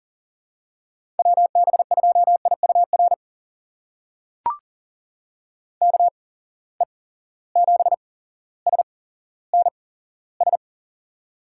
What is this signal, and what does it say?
Morse code, heard as a keyed tone, about 37 wpm, spaced as for 17 wpm, 700 Hz. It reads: WB2IUR KE7SNS
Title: USA Call Signs - 6 Characters - 8x Character Spacing - 40wpm